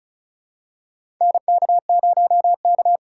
Morse code, heard as a keyed tone, about 34 wpm, 700 Hz. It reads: NK0K
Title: USA Call Signs - 4 Characters - 4x Character Spacing - 35wpm